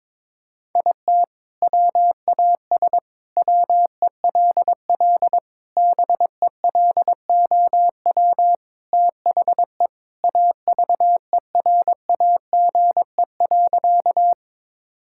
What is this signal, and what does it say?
Morse code, heard as a keyed tone, about 22 wpm, 700 Hz. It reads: IT WAS WELL BELOW THE AVERAGE.
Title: Sentences from Top 1000 Words - Review Part 6 - 22wpm